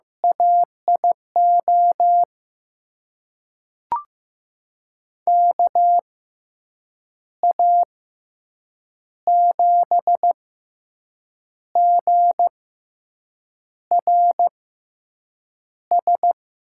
Morse code, heard as a keyed tone, about 16 wpm, spaced as for 6 wpm, 700 Hz. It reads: AIO KA7GRS
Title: USA Call Signs - 6 Characters - 6x Character Spacing - 15wpm